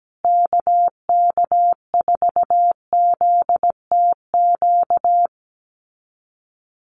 Morse code, heard as a keyed tone, about 17 wpm, 700 Hz. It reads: KK4ZTQ
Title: USA Call Signs Repeated 3x 17wpm